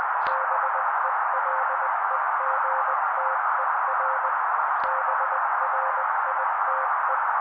Morse code, heard as a keyed tone, about 20 wpm, 530 Hz. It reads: BELEGTER BRITE
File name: daily / 18wpm noise